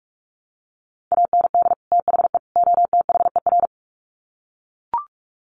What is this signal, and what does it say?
Morse code, heard as a keyed tone, about 44 wpm, 700 Hz. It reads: AND THE OTHER
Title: Sets of 3 Words 45wpm